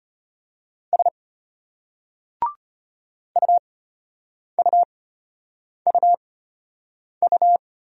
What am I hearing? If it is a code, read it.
Morse code, heard as a keyed tone, about 33 wpm, spaced as for 8 wpm, 700 Hz. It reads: S UUUU